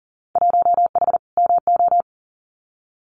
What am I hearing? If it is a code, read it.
Morse code, heard as a keyed tone, about 40 wpm, 700 Hz. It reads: 1H MO
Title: Winter Field Day Exchanges 40wpm